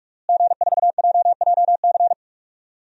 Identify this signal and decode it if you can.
Morse code, heard as a keyed tone, about 41 wpm, 700 Hz. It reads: G4JJC
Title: DXCC-100 Call Signs - 5 Characters - Repeated 2x - 45wpm